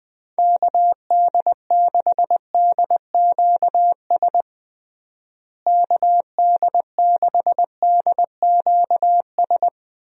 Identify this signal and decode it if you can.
Morse code, heard as a keyed tone, about 20 wpm, 700 Hz. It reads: KD6DQS KD6DQS
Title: USA Call Signs - 6 Characters - Repeated 2x - 20wpm